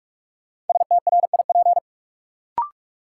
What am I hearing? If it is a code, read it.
Morse code, heard as a keyed tone, about 44 wpm, 700 Hz. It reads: STRIP
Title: Sentences from Top 1600 Words - New Word Drill - 45wpm